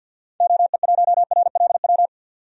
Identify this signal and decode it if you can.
Morse code, heard as a keyed tone, about 49 wpm, 700 Hz. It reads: OE1RLW
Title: DXCC-100 Call Signs - 6 Characters - Repeated 3x - 50wpm